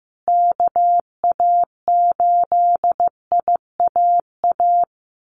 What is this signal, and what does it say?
Morse code, heard as a keyed tone, about 15 wpm, 700 Hz. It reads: KA8IAA
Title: USA Call Signs Repeated 3x 15wpm